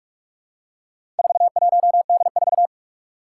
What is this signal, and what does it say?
Morse code, heard as a keyed tone, about 44 wpm, 700 Hz. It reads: 41D4
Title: Four Challenging Letter-Number Combinations 45wpm